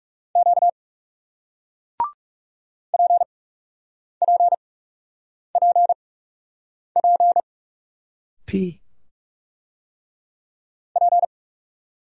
Morse code, heard as a keyed tone, about 40 wpm, spaced as for 8 wpm, 700 Hz. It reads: Q PPPP P